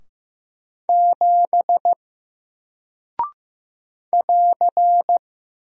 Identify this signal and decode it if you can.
Morse code, heard as a keyed tone, about 15 wpm, 700 Hz. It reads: 7 <AR>
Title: Single Character (Letter:Number:Punctuation:Prosign) Alt 15wpm